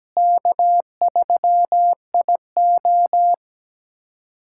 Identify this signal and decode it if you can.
Morse code, heard as a keyed tone, about 17 wpm, 700 Hz. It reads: K3IO
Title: USA Callsign DE Callsign - 4 Characters - 17wpm